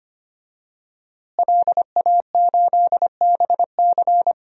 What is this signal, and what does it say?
Morse code, heard as a keyed tone, about 25 wpm, 700 Hz. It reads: LA8BC
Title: DXCC-100 Call Signs - 5 Characters - Repeated 2x - 25wpm